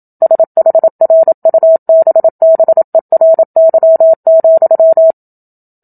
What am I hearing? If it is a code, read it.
Morse code, heard as a keyed tone, about 27 wpm, 650 Hz. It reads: SHRUBBERY,
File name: Book Two / 01-Chapter 1 - 27 WPM